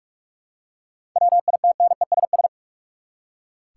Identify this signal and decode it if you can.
Morse code, heard as a keyed tone, about 43 wpm, 700 Hz. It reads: WITNESS
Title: Seven Letter Words - 45wpm